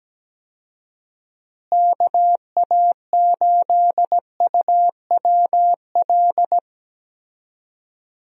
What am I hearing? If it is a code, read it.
Morse code, heard as a keyed tone, about 17 wpm, 700 Hz. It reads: KA8UWL
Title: USA Call Signs - 6 Characters - Repeated 3x - 17wpm